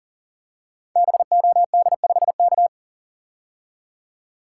Morse code, heard as a keyed tone, about 40 wpm, 700 Hz. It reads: BOD5K
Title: Code Groups - 5 Letter:Numbers - 40wpm